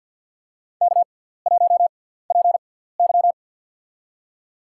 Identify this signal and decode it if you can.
Morse code, heard as a keyed tone, about 51 wpm, spaced as for 17 wpm, 700 Hz. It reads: K1PY
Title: USA Call Signs - 4 Characters - 6x Character Spacing - 50wpm